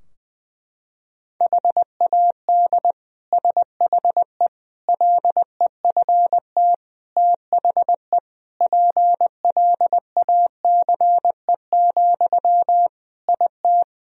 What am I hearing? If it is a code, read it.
Morse code, heard as a keyed tone, about 20 wpm, 700 Hz. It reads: HAD SHE LEFT THE PLACE, IT
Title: Sentences from Top 1000 Words - Review Part 3 - 20wpm